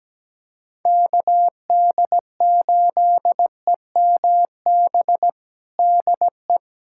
Morse code, heard as a keyed tone, about 17 wpm, 700 Hz. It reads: KD8EMB DE
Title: USA Callsign DE Callsign 17wpm